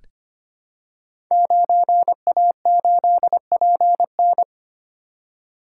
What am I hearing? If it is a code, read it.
Morse code, heard as a keyed tone, about 26 wpm, 700 Hz. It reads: 9A8PN
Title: Code Groups - 5 Letter:Numbers - 25wpm